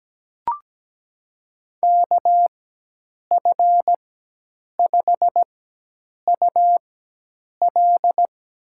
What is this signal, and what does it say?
Morse code, heard as a keyed tone, about 17 wpm, spaced as for 8 wpm, 700 Hz. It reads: KF5UL
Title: USA Call Signs - 4x Character Spacing - 17wpm